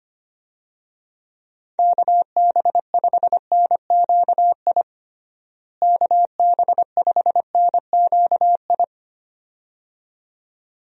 Morse code, heard as a keyed tone, about 25 wpm, 700 Hz. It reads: KB5NQI KB5NQI
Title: USA Call Signs - 6 Characters - Repeated 2x - 25wpm